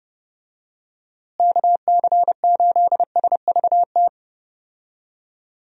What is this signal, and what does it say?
Morse code, heard as a keyed tone, about 30 wpm, 700 Hz. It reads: KC8SVT
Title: USA Call Signs - 6 Characters - Repeated 3x - 30wpm